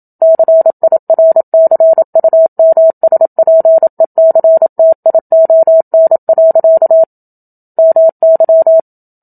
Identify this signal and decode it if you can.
Morse code, heard as a keyed tone, about 27 wpm, 650 Hz. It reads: CIRCUMSPECTION. MY
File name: Book Two / 01-Chapter 1 - 27 WPM